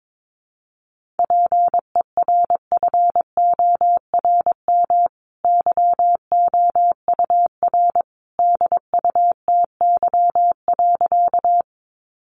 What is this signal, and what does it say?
Morse code, heard as a keyed tone, about 22 wpm, 700 Hz. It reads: PERFORM YOUR DUTY.
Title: Sentences from Top 1500 Words 22wpm